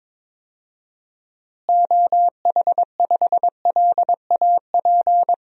Morse code, heard as a keyed tone, about 22 wpm, 700 Hz. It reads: OH5LAP
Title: DXCC-100 Call Signs - 6 Characters - Repeated 3x - 22wpm